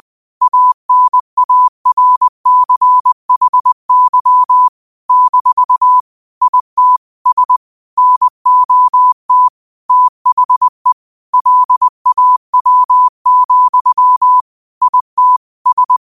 Morse code, heard as a keyed tone, about 20 wpm, 1000 Hz. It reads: ANARCHY - IT S NOT THE LAW, IT S
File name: qotdfiles / 20wpm